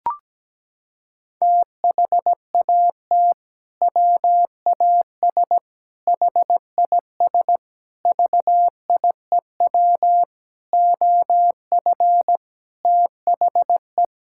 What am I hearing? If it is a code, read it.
Morse code, heard as a keyed tone, about 17 wpm, 700 Hz. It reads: THAT WAS HIS VIEW OF THE
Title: Sentences from Top 1000 Words - Review Part 6 - 17wpm